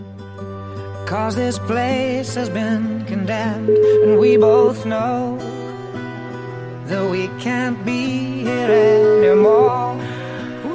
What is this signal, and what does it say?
Ringback tone (Europe, 425 Hz), with noise.